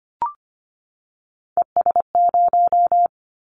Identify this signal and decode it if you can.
Morse code, heard as a keyed tone, about 25 wpm, 700 Hz. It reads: ES0
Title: Callsign Prefixes for Top 100 DXCC - 25wpm